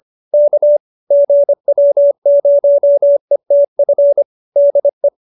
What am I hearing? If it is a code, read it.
Morse code, heard as a keyed tone, about 25 wpm, 570 Hz. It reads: K GW0ETF DE
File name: QSO318 25